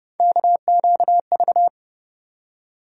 Morse code, heard as a keyed tone, about 31 wpm, 700 Hz. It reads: KQV
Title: USA Call Sign Postfixes - 3 Letters 30wpm